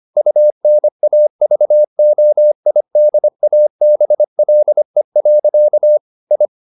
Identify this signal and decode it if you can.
Morse code, heard as a keyed tone, about 25 wpm, 600 Hz. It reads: UNAVOIDABLE. I
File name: the old man and the sea part 1 25wpm